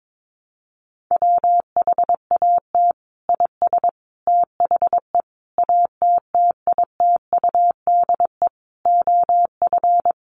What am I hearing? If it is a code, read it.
Morse code, heard as a keyed tone, about 22 wpm, 700 Hz. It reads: WHAT IS THE ATTITUDE OF
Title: Sentences from Top 1300 Words 22wpm